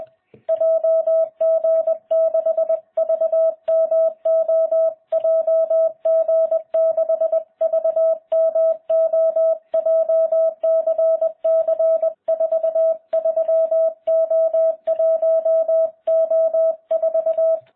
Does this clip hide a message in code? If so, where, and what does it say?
Yes: Morse code from 0.5 to 17.6 s, heard as a keyed tone, about 20 wpm, 650 Hz. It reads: JG6VMO JG6VMO JCC43O1O4